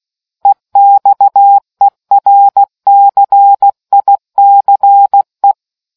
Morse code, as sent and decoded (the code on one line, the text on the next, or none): . -..- . .-. -.-. .. -.-. .
EXERCICE